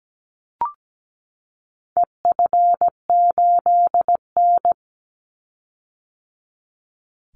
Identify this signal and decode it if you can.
Morse code, heard as a keyed tone, about 17 wpm, 700 Hz. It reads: EF8N
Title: Call Signs - 17wpm